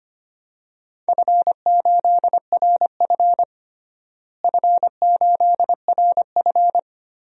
Morse code, heard as a keyed tone, about 25 wpm, 700 Hz. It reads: F8RF F8RF
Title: DXCC-100 Call Signs - 4 Characters - Repeated 2x - 25wpm